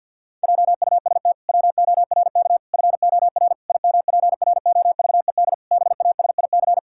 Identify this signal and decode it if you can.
Morse code, heard as a keyed tone, about 50 wpm, 700 Hz. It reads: JUST WORK FOR IMPROVED BASIC
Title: Pangram Challenge v2 - 50wpm